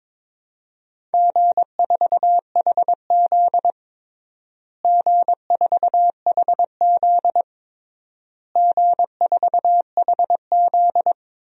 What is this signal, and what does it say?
Morse code, heard as a keyed tone, about 22 wpm, 700 Hz. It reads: G4HZ G4HZ G4HZ
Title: DXCC-100 Call Signs - 4 Characters - Repeated 3x - 22wpm